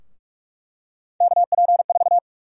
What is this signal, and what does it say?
Morse code, heard as a keyed tone, about 46 wpm, 700 Hz. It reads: KP4
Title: USA Call Sign Prefixes - 45wpm